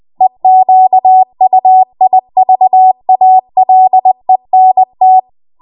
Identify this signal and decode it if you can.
Morse code, heard as a keyed tone, about 20 wpm, 750 Hz. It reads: EQUIVALENT
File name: Morse / 20WPM